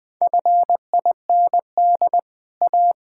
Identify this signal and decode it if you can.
Morse code, heard as a keyed tone, about 20 wpm, 700 Hz. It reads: FIND A
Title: Build A Phrase - 4-Word Phrases - 20wpm